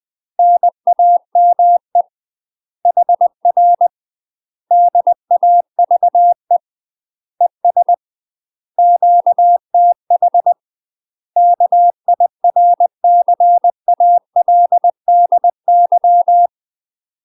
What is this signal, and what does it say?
Morse code, heard as a keyed tone, about 20 wpm, 700 Hz. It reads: NAME HR DAVE ES QTH KIRCALDY
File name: UK MASTER QSOs TRAINING/20WPM